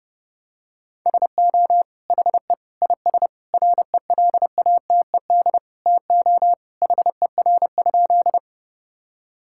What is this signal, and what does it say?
Morse code, heard as a keyed tone, about 32 wpm, 700 Hz. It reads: SO HE IS RELATED TO HER?
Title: Sentences from Top 1000 Words - Review Part 5 - 30wpm